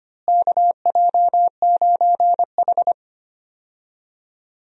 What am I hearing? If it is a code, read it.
Morse code, heard as a keyed tone, about 25 wpm, 700 Hz. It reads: KJ9H
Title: USA Call Signs - 4 Characters - Repeated 2x - 25wpm